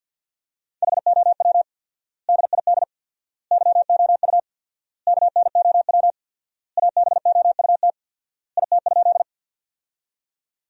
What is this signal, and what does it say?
Morse code, heard as a keyed tone, about 53 wpm, 700 Hz. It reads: HOW DID YOU KNOW ABOUT IT?
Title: 4x Word-Spacing - Sentences from Top 100 Words - 50wpm